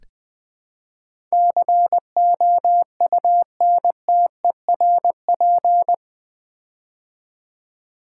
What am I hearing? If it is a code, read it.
Morse code, heard as a keyed tone, about 20 wpm, 700 Hz. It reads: COUNTERP
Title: Build A Word - 4 Letters At A Time - 11-Letter Words - 20wpm